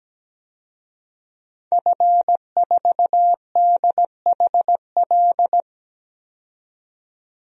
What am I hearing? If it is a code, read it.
Morse code, heard as a keyed tone, about 17 wpm, 700 Hz. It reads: F4DHL
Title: DXCC-100 Call Signs - 5 Characters - 17wpm